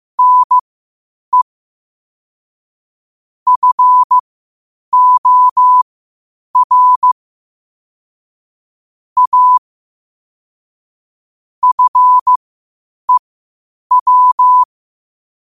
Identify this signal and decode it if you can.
Morse code, heard as a keyed tone, about 15 wpm, spaced as for 8 wpm, 1000 Hz. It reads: NE FOR A FEW
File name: qotdfiles / 8wpm